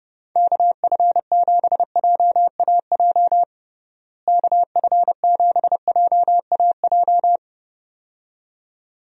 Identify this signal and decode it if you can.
Morse code, heard as a keyed tone, about 31 wpm, 700 Hz. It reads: KF7JAJ KF7JAJ
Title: USA Call Signs - 6 Characters - Repeated 2x - 30wpm